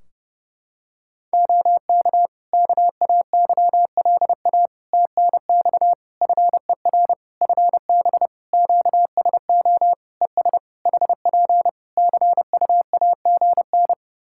Morse code, heard as a keyed tone, about 28 wpm, 700 Hz. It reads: OK KAYLA TNX FER FB QSO ES HP CUAGN
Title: Practice QSO - Exchange 4 - Ending - 30wpm